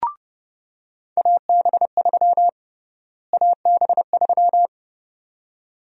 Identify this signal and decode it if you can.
Morse code, heard as a keyed tone, about 30 wpm, 700 Hz. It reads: AB3 AB3
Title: USA Call Sign Prefixes - Repeated 2x - 30wpm